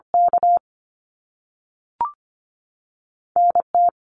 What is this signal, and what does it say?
Morse code, heard as a keyed tone, about 25 wpm, 700 Hz. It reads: K NT
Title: Canadian Provinces and Territories 25wpm